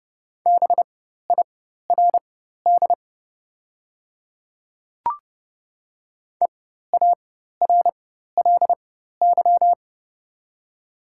Morse code, heard as a keyed tone, about 30 wpm, spaced as for 14 wpm, 700 Hz. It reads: BIRD EARLY
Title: 4x Character Spacing - Top 500 Words - 30wpm